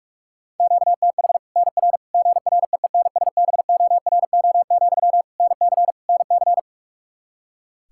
Morse code, heard as a keyed tone, about 46 wpm, 700 Hz. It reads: QTH NR GREENSBORO, NC NC
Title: Practice QSO - Exchange 1 - Beginning - 45wpm